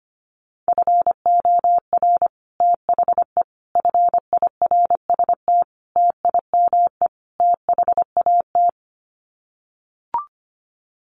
Morse code, heard as a keyed tone, about 25 wpm, 700 Hz. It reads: FOR THE FIRST TIME THAT
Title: Sets of 5 Words Encore 25wpm